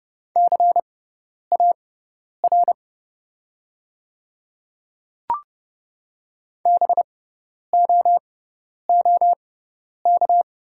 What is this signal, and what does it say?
Morse code, heard as a keyed tone, about 31 wpm, spaced as for 10 wpm, 700 Hz. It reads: CAR BOOK
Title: 6x Character Spacing - Top 500 Words - 30wpm